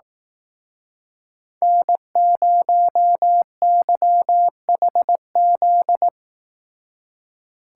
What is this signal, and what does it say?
Morse code, heard as a keyed tone, about 18 wpm, 700 Hz. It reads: N0YHZ
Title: Speed Racing - USA 1x3 Call Signs 15wpm